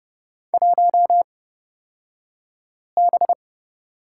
Morse code, heard as a keyed tone, about 29 wpm, spaced as for 5 wpm, 700 Hz. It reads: 1B
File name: Lesson 36d - BT - Two Characters 30wpm